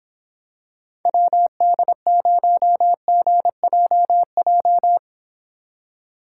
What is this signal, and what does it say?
Morse code, heard as a keyed tone, about 26 wpm, 700 Hz. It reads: WD0GJJ